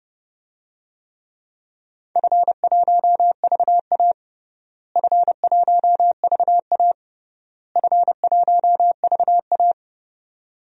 Morse code, heard as a keyed tone, about 30 wpm, 700 Hz. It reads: F1VA F1VA F1VA
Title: DXCC-100 Call Signs - 4 Characters - Repeated 3x - 30wpm